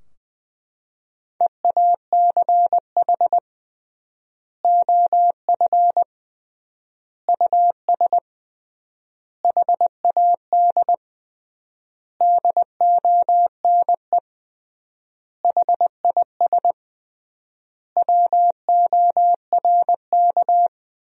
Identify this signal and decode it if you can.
Morse code, heard as a keyed tone, about 20 wpm, 700 Hz. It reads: EACH OF US HAD DONE HIS WORK
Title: 3x Word-Spacing - Sentences from Top 500 Words - 20wpm